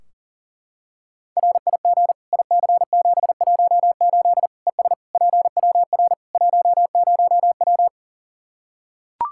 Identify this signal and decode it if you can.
Morse code, heard as a keyed tone, about 40 wpm, 700 Hz. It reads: RIG IC718 ES PWR 10W
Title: Practice QSO - Exchange 2 - Info - 40wpm